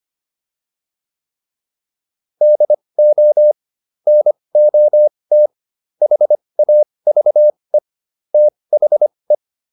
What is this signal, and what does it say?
Morse code, heard as a keyed tone, about 25 wpm, spaced as for 20 wpm, 600 Hz. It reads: DO NOT HAVE THE